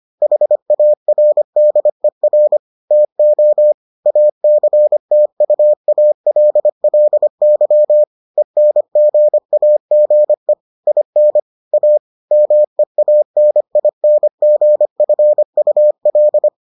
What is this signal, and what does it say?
Morse code, heard as a keyed tone, about 25 wpm, 600 Hz. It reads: HARDER TO ACTUALLY ENGAGE IN A MEANINGFUL